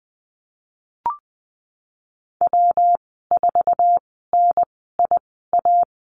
Morse code, heard as a keyed tone, about 20 wpm, spaced as for 15 wpm, 700 Hz. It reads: W4NIA
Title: 179 Lesson 29g - Slash - Call Signs 20wpm